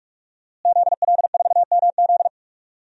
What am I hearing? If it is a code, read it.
Morse code, heard as a keyed tone, about 46 wpm, 700 Hz. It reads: ZL4MZ
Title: DXCC-100 Call Signs - 5 Characters - Repeated 2x - 45wpm